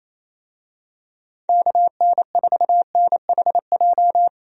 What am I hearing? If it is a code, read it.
Morse code, heard as a keyed tone, about 28 wpm, 700 Hz. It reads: KN4NHJ
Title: USA Call Signs - 6 Characters - Repeated 2x - 28wpm